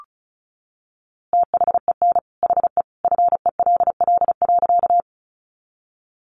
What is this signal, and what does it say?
Morse code, heard as a keyed tone, about 34 wpm, 700 Hz. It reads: THEN HE FELL.
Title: A Sound Of Thunder 35wpm